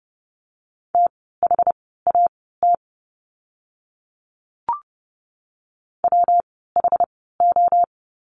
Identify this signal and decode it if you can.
Morse code, heard as a keyed tone, about 30 wpm, spaced as for 17 wpm, 700 Hz. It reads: THAT WHO
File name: course / 30:10wpm / 237 Lesson 38f - Letter q - Words 30wpm